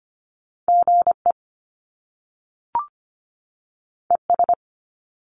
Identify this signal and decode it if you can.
Morse code, heard as a keyed tone, about 25 wpm, 700 Hz. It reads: GE ES
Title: Top Two Letter Combinations in Words 25wpm